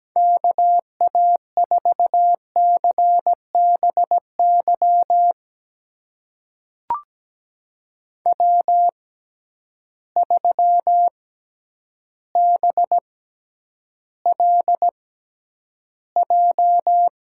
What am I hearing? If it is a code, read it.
Morse code, heard as a keyed tone, about 17 wpm, spaced as for 9 wpm, 700 Hz. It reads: KA4CBY W3BLJ
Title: USA Call Signs - 6x Character Spacing - 17wpm